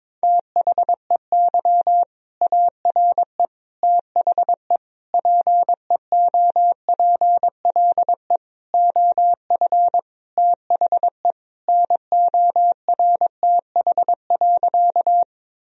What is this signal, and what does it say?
Morse code, heard as a keyed tone, about 22 wpm, 700 Hz. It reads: THEY ARE THE PEOPLE OF THE NORTH.
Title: Sentences from Top 1000 Words - Review Part 1 - 22wpm